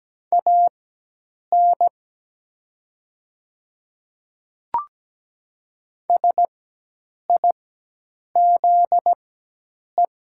Morse code, heard as a keyed tone, about 17 wpm, spaced as for 8 wpm, 700 Hz. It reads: AN SIZE
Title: 4x Character Spacing - Top 500 Words - 17wpm